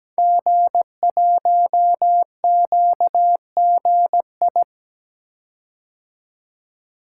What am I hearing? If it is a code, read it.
Morse code, heard as a keyed tone, about 17 wpm, 700 Hz. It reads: G1QGI
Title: Code Groups - 5 Letter:Numbers - 17wpm